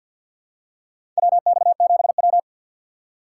Morse code, heard as a keyed tone, about 52 wpm, 700 Hz. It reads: WX7W
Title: USA Call Signs - 4 Characters - 3x Character Spacing - 50wpm